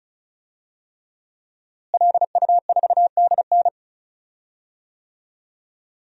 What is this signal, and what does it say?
Morse code, heard as a keyed tone, about 34 wpm, 700 Hz. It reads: LU4DN
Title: DXCC-100 Call Signs - 5 Characters - Repeated 3x - 35wpm